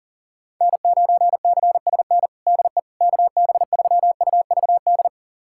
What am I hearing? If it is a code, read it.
Morse code, heard as a keyed tone, about 40 wpm, 700 Hz. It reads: N9CSN DE KB3UVD
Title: USA Callsign DE Callsign 40wpm